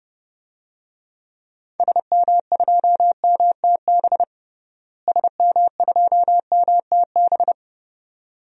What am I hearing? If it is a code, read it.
Morse code, heard as a keyed tone, about 30 wpm, 700 Hz. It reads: SM2MTB SM2MTB
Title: DXCC-100 Call Signs - 6 Characters - Repeated 2x - 30wpm